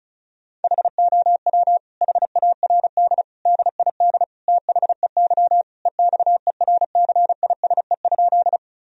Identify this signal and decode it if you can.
Morse code, heard as a keyed tone, about 34 wpm, 700 Hz. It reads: HOW HARD DID THEY EXERCISE?
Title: Sentences from Top 1000 Words - Review Part 4 - 35wpm